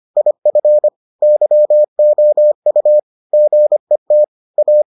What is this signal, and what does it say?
Morse code, heard as a keyed tone, about 25 wpm, 600 Hz. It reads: IF YOU GET A